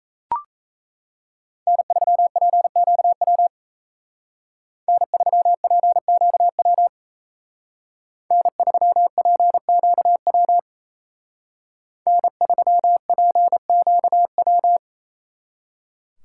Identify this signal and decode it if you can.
Morse code, heard as a keyed tone, about 33 wpm, 700 Hz. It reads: N3PQW N3PQW N3PQW N3PQW
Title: Speed Racing - USA 1x3 Call Signs 28wpm